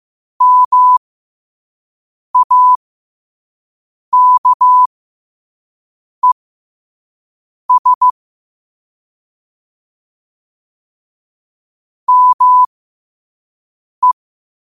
Morse code, heard as a keyed tone, about 15 wpm, spaced as for 5 wpm, 1000 Hz. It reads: MAKES ME
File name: qotdfiles / 5wpm